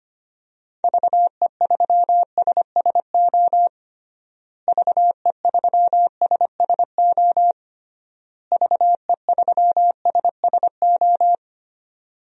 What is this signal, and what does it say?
Morse code, heard as a keyed tone, about 25 wpm, 700 Hz. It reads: VE3SSO VE3SSO VE3SSO
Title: DXCC-100 Call Signs - 6 Characters - Repeated 3x - 25wpm